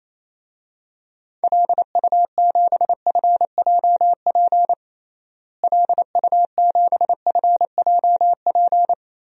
Morse code, heard as a keyed tone, about 28 wpm, 700 Hz. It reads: LU7FJP LU7FJP
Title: DXCC-100 Call Signs - 6 Characters - Repeated 3x - 28wpm